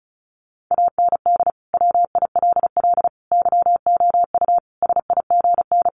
Morse code, heard as a keyed tone, about 34 wpm, 700 Hz. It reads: AND WILL YOU SIGN
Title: Sentences from Top 1400 Words 35wpm